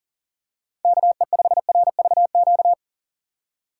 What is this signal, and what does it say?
Morse code, heard as a keyed tone, about 40 wpm, 700 Hz. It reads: KE5RVQ
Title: USA Call Signs - 6 Characters - 2x Character Spacing - 40wpm